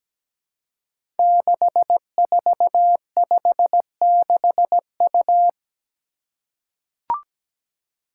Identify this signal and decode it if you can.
Morse code, heard as a keyed tone, about 17 wpm, 700 Hz. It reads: 6456U
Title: Five Challenging Letter-Number Combinations 17wpm